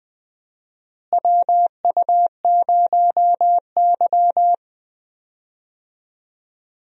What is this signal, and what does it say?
Morse code, heard as a keyed tone, about 20 wpm, 700 Hz. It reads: WU0Y
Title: USA Call Signs - 4 Characters - Repeated 2x - 20wpm